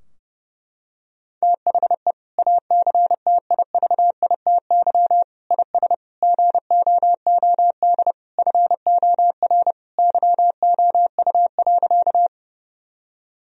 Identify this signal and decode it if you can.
Morse code, heard as a keyed tone, about 30 wpm, 700 Hz. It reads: THE ACTIVITY IS GOOD FOR YOU.
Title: Sentences from Top 1000 Words - Review Part 4 - 30wpm